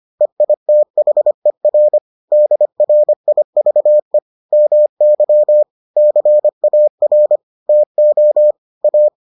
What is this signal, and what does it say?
Morse code, heard as a keyed tone, about 25 wpm, 600 Hz. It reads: EITHER DRIVE MY CAR TO A